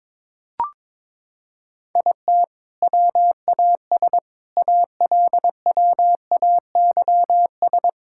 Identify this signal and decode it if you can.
Morse code, heard as a keyed tone, about 22 wpm, 700 Hz. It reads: IT WAS ALWAYS
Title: Sentences from Top 1000 Words - Review Part 6 - 22wpm